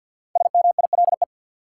Morse code, heard as a keyed tone, about 51 wpm, 700 Hz. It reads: SMILE
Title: Five Letter Words - 50wpm